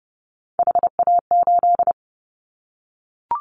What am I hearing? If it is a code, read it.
Morse code, heard as a keyed tone, about 30 wpm, 700 Hz. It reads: HA8
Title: Callsign Prefixes for Top 25 DXCC - 30wpm